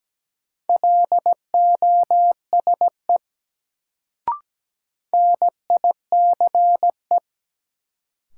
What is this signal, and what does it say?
Morse code, heard as a keyed tone, about 17 wpm, 700 Hz. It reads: LOSE NICE
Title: Sentences from Top 1100 Words - New Word Drill - 17wpm